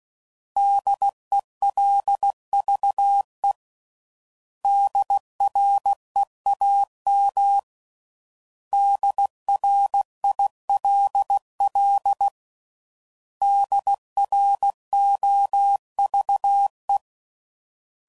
Morse code, heard as a keyed tone, about 16 wpm, 800 Hz. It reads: DELVE DREAM DRILL DROVE
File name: Prefixes / By First Letter / Ds